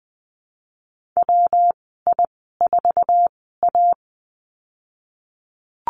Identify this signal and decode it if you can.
Morse code, heard as a keyed tone, about 20 wpm, spaced as for 14 wpm, 700 Hz. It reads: WI4A
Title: 128 Lesson 21g - Number 6 - Call Signs 20wpm